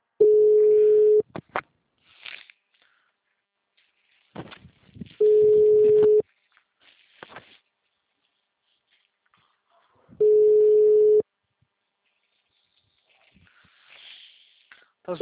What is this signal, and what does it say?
Ringback tone (Europe, 425 Hz).
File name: out- -2007- - - .230